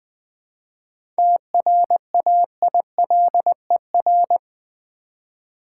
Morse code, heard as a keyed tone, about 20 wpm, 700 Hz. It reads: TRAILER
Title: 7 Letter Words - First 3 Letters Given - 20wpm